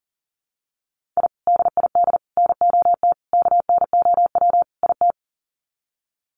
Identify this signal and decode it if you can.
Morse code, heard as a keyed tone, about 40 wpm, 700 Hz. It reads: I DID NOT KNOW IT
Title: Sets of 5 Words Encore2 40wpm